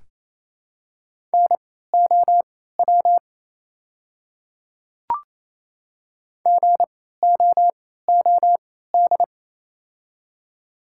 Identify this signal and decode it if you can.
Morse code, heard as a keyed tone, about 28 wpm, spaced as for 16 wpm, 700 Hz. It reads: NOW GOOD
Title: 3x Character Spacing - Top 300 Words - 28wpm